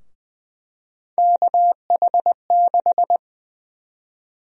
Morse code, heard as a keyed tone, about 20 wpm, 700 Hz. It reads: KH6
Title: USA Call Sign Prefixes - Repeated 2x - 20wpm